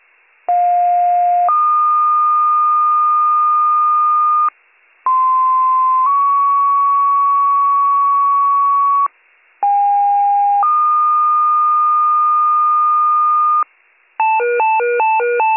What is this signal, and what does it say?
Two-tone page, heard as a one-second tone, then a three-second tone, about 710 then 1220 Hz, about 1040 then 1120 Hz, about 810 then 1220 Hz. It is heard from 0.5 to 13.6 s.